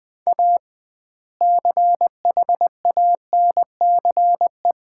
Morse code, heard as a keyed tone, about 20 wpm, 700 Hz. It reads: A CHANCE
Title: 2x Word-Spacing - Sets of 3 Words - 20wpm